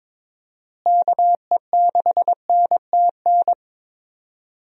Morse code, heard as a keyed tone, about 21 wpm, 700 Hz. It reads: KE6NTN
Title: USA Call Signs - 6 Characters - Repeated 3x - 22wpm